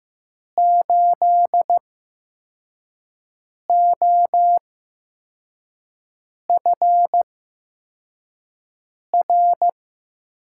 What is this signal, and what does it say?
Morse code, heard as a keyed tone, about 16 wpm, spaced as for 4 wpm, 700 Hz. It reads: 8OFR
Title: USA Call Signs - 6 Characters - 8x Character Spacing - 15wpm